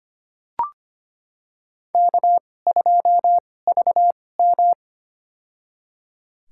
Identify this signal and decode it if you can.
Morse code, heard as a keyed tone, about 25 wpm, spaced as for 18 wpm, 700 Hz. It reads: K2VM
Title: USA Call Signs - 4 Characters - 2x Character Spacing - 25wpm